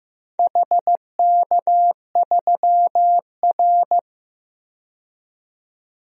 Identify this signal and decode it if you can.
Morse code, heard as a keyed tone, about 15 wpm, 700 Hz. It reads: HK3R
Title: DXCC-100 Call Signs - 4 Characters - 15wpm